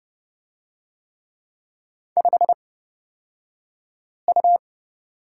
Morse code, heard as a keyed tone, about 30 wpm, spaced as for 5 wpm, 700 Hz. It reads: 5U
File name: Lesson 38d - Letter q - Two Characters 30wpm